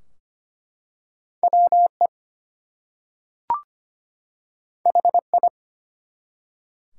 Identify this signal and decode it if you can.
Morse code, heard as a keyed tone, about 25 wpm, 700 Hz. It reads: WE HI
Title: Two Letter Words - 25wpm